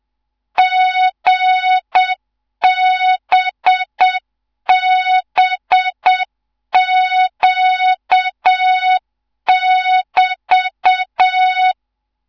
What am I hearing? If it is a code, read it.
Morse code, heard as a keyed tone, about 7 wpm, 3700 Hz. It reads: GBBQ<BT>
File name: Morze-HA2MN / hang / 14 280 HA2MN